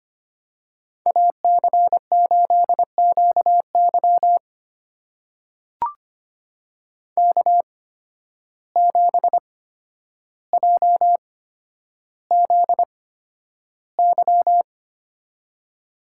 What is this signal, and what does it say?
Morse code, heard as a keyed tone, about 25 wpm, spaced as for 11 wpm, 700 Hz. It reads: AC8QY K7JZY
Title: USA Call Signs - 5 Characters - 8x Character Spacing - 25wpm